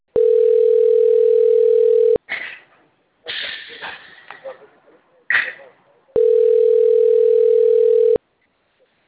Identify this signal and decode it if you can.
Ringback tone (North America), 440 + 480 Hz, with speech.
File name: out- - - - - .37046